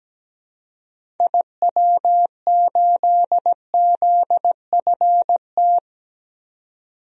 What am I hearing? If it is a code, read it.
Morse code, heard as a keyed tone, about 17 wpm, 700 Hz. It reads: IW8ZFT
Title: DXCC-100 Call Signs - 6 Characters - 17wpm